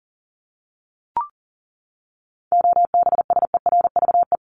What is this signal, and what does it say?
Morse code, heard as a keyed tone, about 40 wpm, 700 Hz. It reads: OBSERVE
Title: ED Suffix - Top 100 Words with Base Words 40wpm